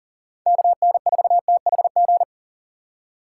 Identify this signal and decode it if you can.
Morse code, heard as a keyed tone, about 40 wpm, 700 Hz. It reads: KN4THG
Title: USA Call Signs - 6x Character Spacing - 40wpm